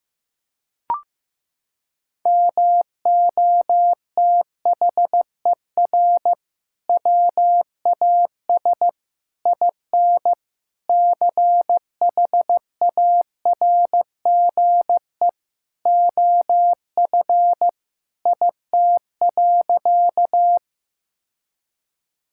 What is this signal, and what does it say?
Morse code, heard as a keyed tone, about 15 wpm, 700 Hz. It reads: MOTHER WAS IN CHARGE OF IT.
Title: Sentences from Top 1000 Words - Review Part 6 - 15wpm